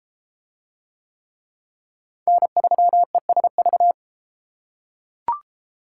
Morse code, heard as a keyed tone, about 33 wpm, 700 Hz. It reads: N3ESV